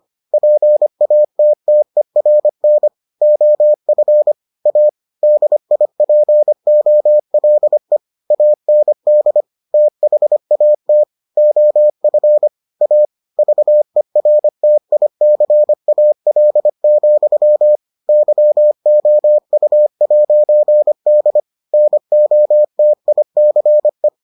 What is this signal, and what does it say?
Morse code, heard as a keyed tone, about 25 wpm, 600 Hz. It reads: PATTERN OF A DIPOLE AND THAT OF A VERTICAL, YOU'D NOTICE